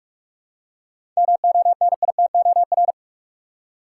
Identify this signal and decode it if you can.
Morse code, heard as a keyed tone, about 45 wpm, 700 Hz. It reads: MONITOR
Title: Seven Letter Words - 45wpm